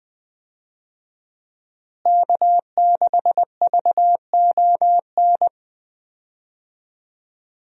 Morse code, heard as a keyed tone, about 20 wpm, 700 Hz. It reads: K6VON
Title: USA Call Signs - 5 Characters - Repeated 3x - 20wpm